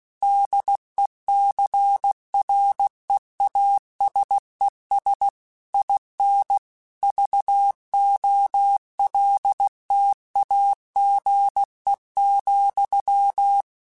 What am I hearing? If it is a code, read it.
Morse code, heard as a keyed tone, about 16 wpm, 800 Hz. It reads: DECREASES IN VOLTAGE,
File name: Technical Subjects / Technical Subjects-mp3s / XC @15